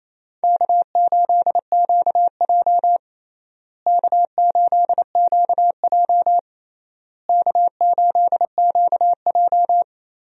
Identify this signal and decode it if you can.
Morse code, heard as a keyed tone, about 28 wpm, 700 Hz. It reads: K8QJ K8QJ K8QJ
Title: USA Call Signs - 4 Characters - Repeated 3x - 28wpm